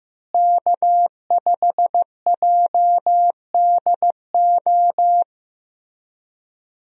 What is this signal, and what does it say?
Morse code, heard as a keyed tone, about 16 wpm, 700 Hz. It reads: K5JDO
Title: USA Call Signs - 5 Characters - Repeated 3x - 15wpm